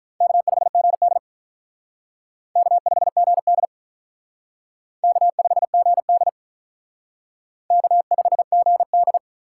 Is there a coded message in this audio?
Yes: Morse code from 0.2 to 9.2 s, heard as a keyed tone, about 45 wpm, 700 Hz. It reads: K5GD K5GD K5GD K5GD